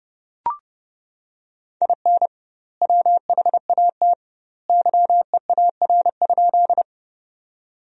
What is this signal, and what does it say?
Morse code, heard as a keyed tone, about 32 wpm, 700 Hz. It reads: IN WHAT YEAR?
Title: 2x Word-Spacing - Sentences from Top 200 Words - 30wpm